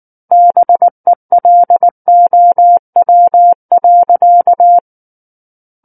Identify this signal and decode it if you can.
Morse code, heard as a keyed tone, about 19 wpm, 700 Hz. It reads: BELOW.
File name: Book One / 10-Chapter 10 - 19 WPM